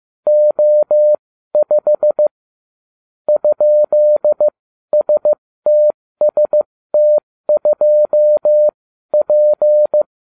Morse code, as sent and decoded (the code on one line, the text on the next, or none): --- ..... / ..--.. ... - ... - ..--- .--.
O5 ?STST2P